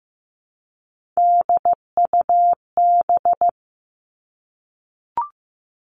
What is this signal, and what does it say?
Morse code, heard as a keyed tone, about 15 wpm, 700 Hz. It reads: DUB
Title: Three Challenging Letter-Number Combinations 15wpm